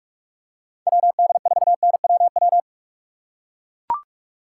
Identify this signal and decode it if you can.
Morse code, heard as a keyed tone, about 44 wpm, 700 Hz. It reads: WD4NWW
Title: USA Call Signs - 6 Characters - 6x Character Spacing - 45wpm